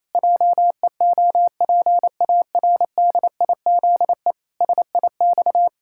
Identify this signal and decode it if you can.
Morse code, heard as a keyed tone, about 27 wpm, 700 Hz. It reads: JEOPARDIZE SIX
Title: Pangram Challenge v2 - 28wpm